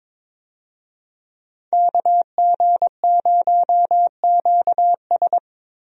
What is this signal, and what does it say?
Morse code, heard as a keyed tone, about 21 wpm, 700 Hz. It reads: KG0QS
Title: USA Call Signs - 5 Characters - Repeated 2x - 22wpm